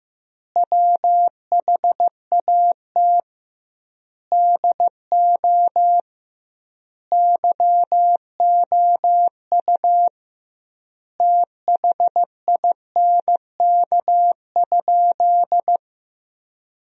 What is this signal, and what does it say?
Morse code, heard as a keyed tone, about 15 wpm, 700 Hz. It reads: WHAT DO YOU THINK?
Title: 2x Word-Spacing - Sentences from Top 200 Words - 15wpm